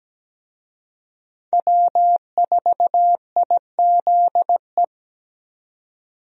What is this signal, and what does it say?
Morse code, heard as a keyed tone, about 17 wpm, 700 Hz. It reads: W4IZE
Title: USA Call Signs - 5 Characters - 17wpm